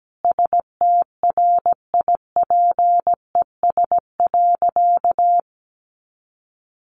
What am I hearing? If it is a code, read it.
Morse code, heard as a keyed tone, about 17 wpm, 700 Hz. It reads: STRIPES.
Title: Additional Easy Phrases 17wpm